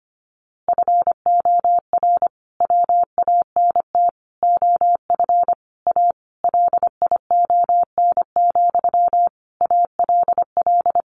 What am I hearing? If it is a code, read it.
Morse code, heard as a keyed tone, about 25 wpm, 700 Hz. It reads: FOR WANT OF A LION, ALL
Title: A Sound Of Thunder 25wpm